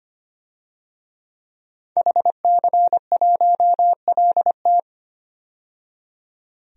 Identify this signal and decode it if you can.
Morse code, heard as a keyed tone, about 25 wpm, 700 Hz. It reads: HC1LT
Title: DXCC-100 Call Signs - 5 Characters - 25wpm